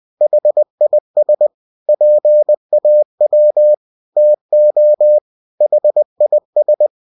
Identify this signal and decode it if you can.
Morse code, heard as a keyed tone, about 20 wpm, 600 Hz. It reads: HIS PAW TO HIS
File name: cw-books / winnie-the-pooh / winnie-20wpm-page-0005